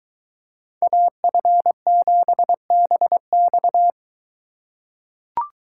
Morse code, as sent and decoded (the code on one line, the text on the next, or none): .- ..-. --... -... -..-
AF7BX